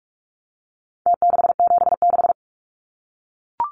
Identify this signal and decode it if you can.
Morse code, heard as a keyed tone, about 44 wpm, 700 Hz. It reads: T676
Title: Numbers - Contest Style - 4-Digit 45wpm